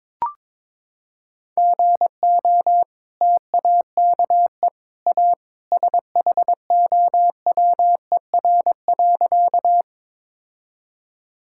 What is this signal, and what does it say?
Morse code, heard as a keyed tone, about 22 wpm, 700 Hz. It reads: GO TAKE A SHOWER.
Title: Sentences from Top 1600 Words 22wpm